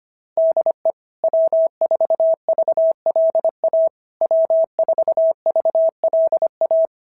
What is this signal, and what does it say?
Morse code, heard as a keyed tone, about 25 wpm, 660 Hz. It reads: DE W4VLA W4VLA
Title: SOTA QSO Practice 25wpm